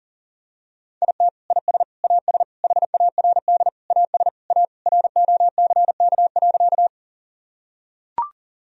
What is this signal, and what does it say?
Morse code, heard as a keyed tone, about 40 wpm, 700 Hz. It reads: IT IS AS HARD AS A ROCK.
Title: Sentences from Top 1000 Words - Review Part 2 - 40wpm